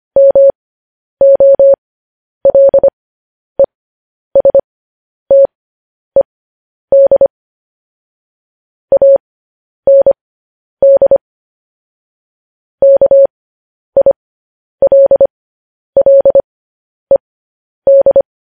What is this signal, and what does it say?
Morse code, heard as a keyed tone, about 25 wpm, spaced as for 10 wpm, 550 Hz. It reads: MOLESTED AND KILLED